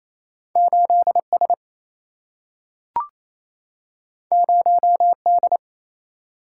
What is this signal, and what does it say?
Morse code, heard as a keyed tone, about 28 wpm, 700 Hz. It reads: 8 S 0 D
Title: Single-Letter Single-Number 28wpm